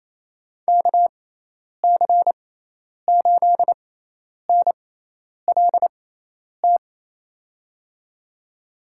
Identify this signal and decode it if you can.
Morse code, heard as a keyed tone, about 27 wpm, spaced as for 10 wpm, 700 Hz. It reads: KC8NLT
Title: USA Call Signs - 6 Characters - 6x Character Spacing - 28wpm